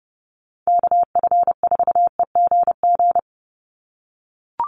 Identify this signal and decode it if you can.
Morse code, heard as a keyed tone, about 30 wpm, 700 Hz. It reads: KF4EGG
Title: USA Call Signs 30wpm